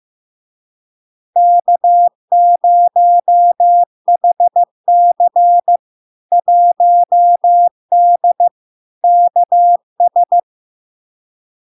Morse code, heard as a keyed tone, about 15 wpm, 700 Hz. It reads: K0HC 1D KS
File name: LARK-FD-2022-Log-in-CW-15wpm-700-Hz